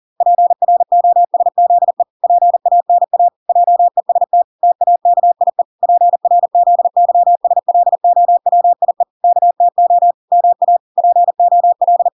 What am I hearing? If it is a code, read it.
Morse code, heard as a keyed tone, about 40 wpm, 700 Hz. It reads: PROSZE PANA JEST TAKIE PRZYSLOWIE KTO MA POL